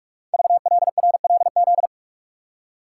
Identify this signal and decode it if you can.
Morse code, heard as a keyed tone, about 44 wpm, 700 Hz. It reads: VLRLZ
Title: Code Groups - 5 Letters - 45wpm